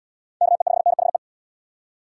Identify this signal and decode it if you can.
Morse code, heard as a keyed tone, about 72 wpm, 700 Hz. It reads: KE5NHE